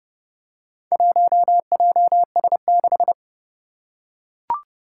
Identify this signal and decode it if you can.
Morse code, heard as a keyed tone, about 30 wpm, 700 Hz. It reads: 1JS6
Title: Four Challenging Letter-Number Combinations 30wpm